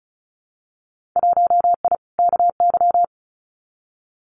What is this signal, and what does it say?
Morse code, heard as a keyed tone, about 36 wpm, 690 Hz. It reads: 1I KY
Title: Winter Field Day Exchanges 35wpm